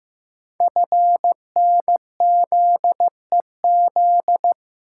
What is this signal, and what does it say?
Morse code, heard as a keyed tone, about 15 wpm, 700 Hz. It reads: FNZEZ
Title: Code Groups - 5 Letters - 15wpm